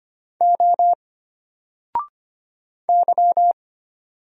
Morse code, heard as a keyed tone, about 25 wpm, 700 Hz. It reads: O Y
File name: Single-Letter-Number-25wpm